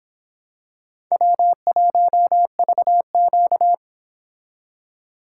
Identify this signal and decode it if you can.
Morse code, heard as a keyed tone, about 25 wpm, 700 Hz. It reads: W1VQ